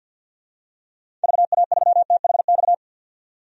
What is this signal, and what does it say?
Morse code, heard as a keyed tone, about 51 wpm, 700 Hz. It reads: VA3THX
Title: DXCC-100 Call Signs - 6 Characters - 50wpm